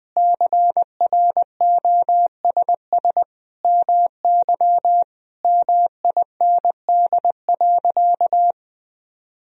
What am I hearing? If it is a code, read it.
Morse code, heard as a keyed tone, about 20 wpm, 700 Hz. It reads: CROSS MY MIND.
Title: Sentences from Top 1000 Words - Review Part 1 - 20wpm